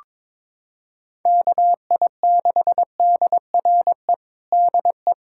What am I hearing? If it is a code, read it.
Morse code, heard as a keyed tone, about 22 wpm, 700 Hz. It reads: KI6DRE DE
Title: USA Callsign DE Callsign 22wpm